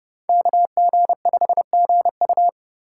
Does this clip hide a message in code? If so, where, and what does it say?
Yes: Morse code from 0.3 to 2.5 s, heard as a keyed tone, about 29 wpm, 700 Hz. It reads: KG5GU